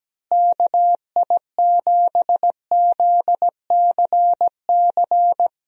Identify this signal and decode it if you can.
Morse code, heard as a keyed tone, about 17 wpm, 700 Hz. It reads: KI7ZCC
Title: USA Call Signs - 6 Characters - Repeated 2x - 17wpm